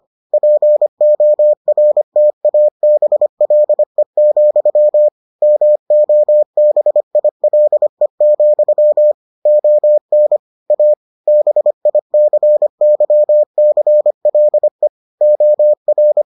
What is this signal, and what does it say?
Morse code, heard as a keyed tone, about 25 wpm, 600 Hz. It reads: PORTABLE, MOBILE, ON A BICYCLE OR